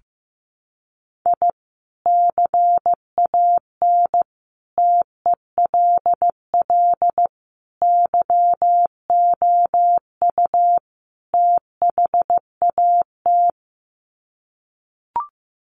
Morse code, heard as a keyed tone, about 15 wpm, 700 Hz. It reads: I CAN TELL YOU THAT
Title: Sets of 5 Words 15wpm